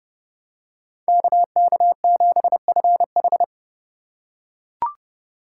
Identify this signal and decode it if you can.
Morse code, heard as a keyed tone, about 28 wpm, 700 Hz. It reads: KK7FH